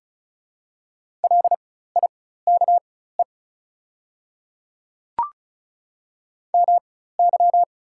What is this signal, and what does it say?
Morse code, heard as a keyed tone, about 34 wpm, spaced as for 16 wpm, 700 Hz. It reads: LIKE MY
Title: 4x Character Spacing - Top 100 Words - 35wpm